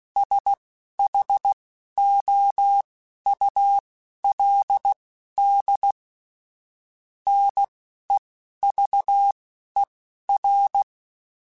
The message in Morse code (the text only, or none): SHOULD NEVER